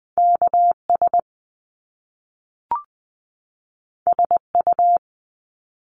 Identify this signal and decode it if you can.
Morse code, heard as a keyed tone, about 20 wpm, 700 Hz. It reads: K S S U
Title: Top Two Letter Combinations in Words 20wpm